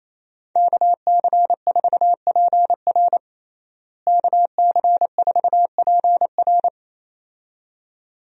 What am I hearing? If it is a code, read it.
Morse code, heard as a keyed tone, about 27 wpm, 700 Hz. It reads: KC4PR KC4PR
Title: USA Call Signs - 5 Characters - Repeated 2x - 28wpm